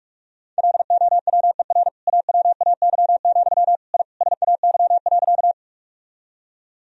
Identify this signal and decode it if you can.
Morse code, heard as a keyed tone, about 46 wpm, 700 Hz. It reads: LOWER AWAY, I SAY.
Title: Sentences from Top 1000 Words - Review Part 4 - 45wpm